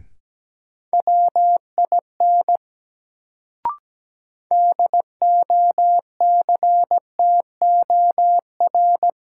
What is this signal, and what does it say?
Morse code, heard as a keyed tone, about 17 wpm, 700 Hz. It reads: WIN DOCTOR
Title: Sentences from Top 1500 Words - New Word Drill - 17wpm